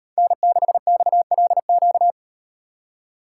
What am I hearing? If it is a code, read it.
Morse code, heard as a keyed tone, about 36 wpm, 700 Hz. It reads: N6XLQ